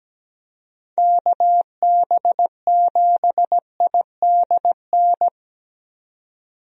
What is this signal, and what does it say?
Morse code, heard as a keyed tone, about 17 wpm, 700 Hz. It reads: KB7IDN
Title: USA Call Signs - 6 Characters - 17wpm